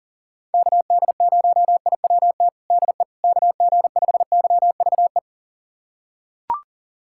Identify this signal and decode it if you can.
Morse code, heard as a keyed tone, about 40 wpm, 700 Hz. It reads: KD0IWT DE KG5YVE
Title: USA Callsign DE Callsign - 6 Characters - 40wpm